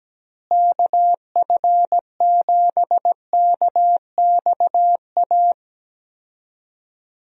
Morse code, heard as a keyed tone, about 17 wpm, 700 Hz. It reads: KF7KXA
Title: USA Call Signs - 6 Characters - Repeated 3x - 17wpm